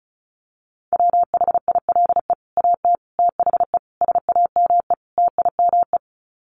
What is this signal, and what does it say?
Morse code, heard as a keyed tone, about 34 wpm, 700 Hz. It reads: WHILE AT THE SAME TIME
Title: Sets of 5 Words 35wpm